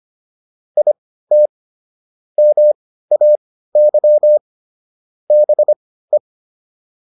Morse code, heard as a keyed tone, about 25 wpm, spaced as for 15 wpm, 600 Hz. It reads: IT MAY BE